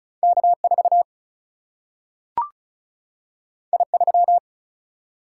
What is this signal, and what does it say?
Morse code, heard as a keyed tone, about 36 wpm, 700 Hz. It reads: K 4 I 3
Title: Single-Letter Single-Number 35wpm